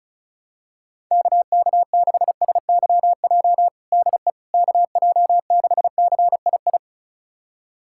Morse code, heard as a keyed tone, about 34 wpm, 690 Hz. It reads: KK6SYJ DE KJ6CII
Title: USA Callsign DE Callsign - 6 Characters - 35wpm